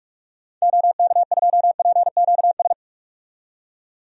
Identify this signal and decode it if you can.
Morse code, heard as a keyed tone, about 41 wpm, 700 Hz. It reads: OK2PQS
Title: DXCC-100 Call Signs - 6 Characters - 45wpm